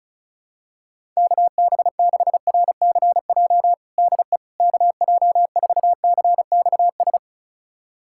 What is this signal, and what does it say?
Morse code, heard as a keyed tone, about 36 wpm, 700 Hz. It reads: KB6RCJ DE KJ4CXS
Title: USA Callsign DE Callsign - 6 Characters - 35wpm